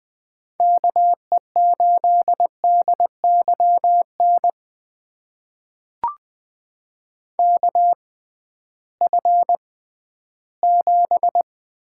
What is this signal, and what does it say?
Morse code, heard as a keyed tone, about 20 wpm, spaced as for 13 wpm, 700 Hz. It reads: KE8DYN KF7
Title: USA Call Signs - 6 Characters - 6x Character Spacing - 20wpm